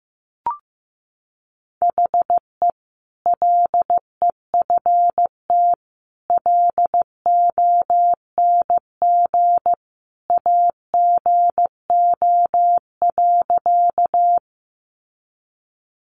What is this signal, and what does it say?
Morse code, heard as a keyed tone, about 15 wpm, 700 Hz. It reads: HE LEFT LONG AGO.
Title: Sentences from Top 500 Words 15wpm